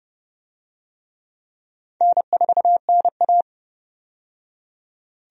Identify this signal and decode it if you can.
Morse code, heard as a keyed tone, about 30 wpm, 700 Hz. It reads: N4NA
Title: USA Call Signs - 4 Characters - 30wpm